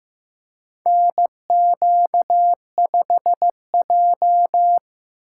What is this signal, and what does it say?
Morse code, heard as a keyed tone, about 15 wpm, 700 Hz. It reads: NQ5J
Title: USA Call Signs - 4 Characters - 3x Character Spacing - 15wpm